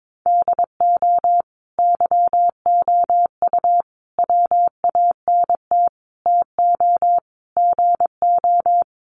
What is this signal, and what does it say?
Morse code, heard as a keyed tone, about 22 wpm, 700 Hz. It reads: DO YOU WANT TO GO
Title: Sentences from Top 1400 Words 22wpm